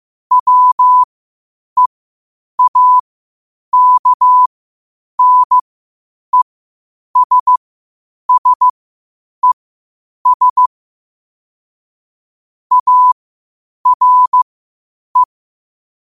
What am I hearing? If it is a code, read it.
Morse code, heard as a keyed tone, about 15 wpm, spaced as for 8 wpm, 1000 Hz. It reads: WEAKNESSES ARE
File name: qotdfiles / 8wpm